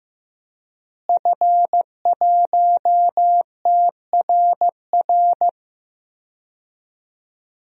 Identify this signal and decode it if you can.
Morse code, heard as a keyed tone, about 16 wpm, 700 Hz. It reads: F1TRR
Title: DXCC-100 Call Signs - 5 Characters - Repeated 3x - 15wpm